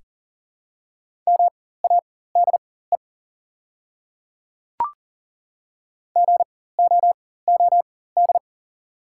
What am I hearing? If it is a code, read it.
Morse code, heard as a keyed tone, about 40 wpm, spaced as for 19 wpm, 700 Hz. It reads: MADE GOOD
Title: 4x Character Spacing - Top 200 Words - 40wpm